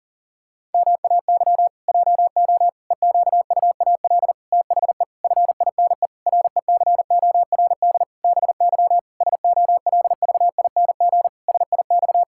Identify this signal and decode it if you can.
Morse code, heard as a keyed tone, about 40 wpm, 700 Hz. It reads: MAY JO EQUAL THE FINE RECORD BY SOLVING SIX
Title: Pangram Challenge v2 - 40wpm